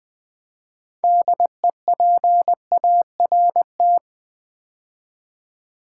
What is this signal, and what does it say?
Morse code, heard as a keyed tone, about 20 wpm, 690 Hz. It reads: DEPART
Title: Build A Word - 2 Letters At A Time - 6-Letter Words - 20wpm